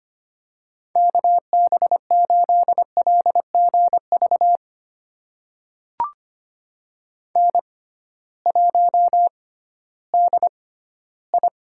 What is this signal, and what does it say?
Morse code, heard as a keyed tone, about 25 wpm, spaced as for 14 wpm, 700 Hz. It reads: KB8LGV N1DI
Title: USA Call Signs - 6x Character Spacing - 25wpm